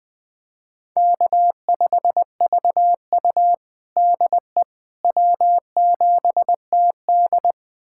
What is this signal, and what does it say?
Morse code, heard as a keyed tone, about 21 wpm, 700 Hz. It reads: K5VU DE W7TD
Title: USA Callsign DE Callsign - 4 Characters - 20wpm